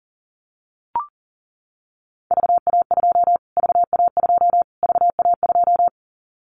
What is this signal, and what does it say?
Morse code, heard as a keyed tone, about 40 wpm, 700 Hz. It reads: VA2 VA2 VA2
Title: Callsign Prefixes for Top 25 DXCC - Repeated 3x - 40wpm